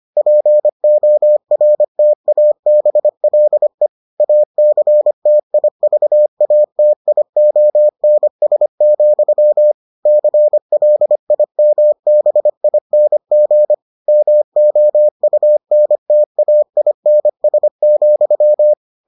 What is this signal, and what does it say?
Morse code, heard as a keyed tone, about 25 wpm, 600 Hz. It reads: PORTABLE ACTIVATIONS, CLIMBING MOUNTAINS,